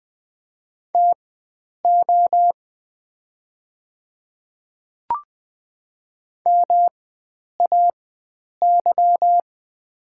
Morse code, heard as a keyed tone, about 20 wpm, spaced as for 9 wpm, 700 Hz. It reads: TO MAY
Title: 4x Character Spacing - Top 100 Words - 20wpm